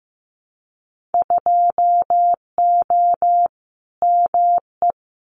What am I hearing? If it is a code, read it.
Morse code, heard as a keyed tone, about 15 wpm, 700 Hz. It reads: 2O ME
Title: Winter Field Day Exchanges Encore 15wpm